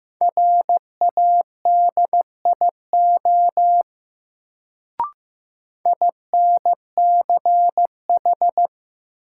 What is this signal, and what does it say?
Morse code, heard as a keyed tone, about 16 wpm, 700 Hz. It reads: RADIO INCH
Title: Sentences from Top 1000 Words - Review Part 3 - New Word Drill - 15wpm